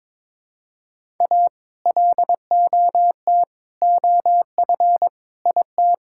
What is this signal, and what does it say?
Morse code, heard as a keyed tone, about 22 wpm, 700 Hz. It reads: A LOT OF IT
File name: practice / 22wpm / Sentences with 8-Word Phrases - 22wpm